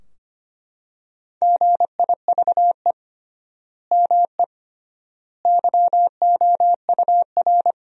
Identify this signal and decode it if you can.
Morse code, heard as a keyed tone, about 25 wpm, 700 Hz. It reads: GIVE ME YOUR
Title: 3x Word-Spacing - Sentences from Top 200 Words - 25wpm